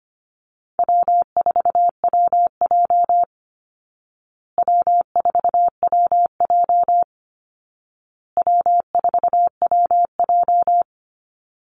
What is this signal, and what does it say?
Morse code, heard as a keyed tone, about 25 wpm, 700 Hz. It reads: W4WJ W4WJ W4WJ
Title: Call Signs Repeated 2x - 25wpm